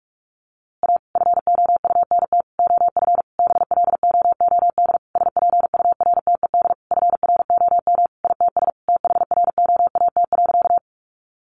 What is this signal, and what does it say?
Morse code, heard as a keyed tone, about 44 wpm, 700 Hz. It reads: A FOUNT OF BLOOD SPURTED FROM ITS THROAT.
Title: A Sound Of Thunder 45wpm